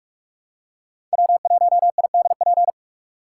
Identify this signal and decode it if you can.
Morse code, heard as a keyed tone, about 44 wpm, 700 Hz. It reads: W1IDP
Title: USA Call Signs - 6x Character Spacing - 45wpm